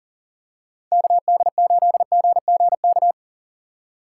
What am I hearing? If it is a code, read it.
Morse code, heard as a keyed tone, about 40 wpm, 700 Hz. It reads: KD8GGK
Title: USA Call Signs - 6 Characters - 3x Character Spacing - 40wpm